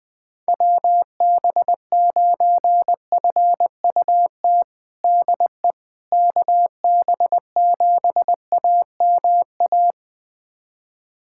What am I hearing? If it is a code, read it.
Morse code, heard as a keyed tone, about 21 wpm, 700 Hz. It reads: WB9FUT DE KB7AMA
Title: USA Callsign DE Callsign - 6 Characters - 20wpm